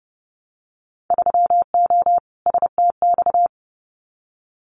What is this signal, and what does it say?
Morse code, heard as a keyed tone, about 30 wpm, 700 Hz. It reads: 3O STX
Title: Winter Field Day Exchanges 30wpm